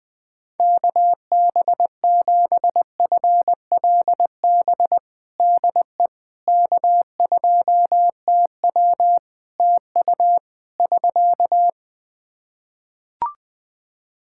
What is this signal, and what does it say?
Morse code, heard as a keyed tone, about 20 wpm, 700 Hz. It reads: KB7FLB DE K2TW TU <SK>
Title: Practice QSO - Exchange 4 - Ending - 20wpm